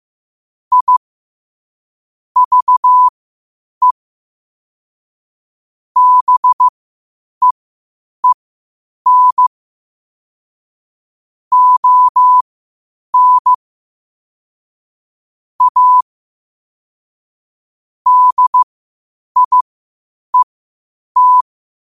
Morse code, heard as a keyed tone, about 15 wpm, spaced as for 8 wpm, 1000 Hz. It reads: I VE BEEN ON A DIET
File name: qotdfiles / 8wpm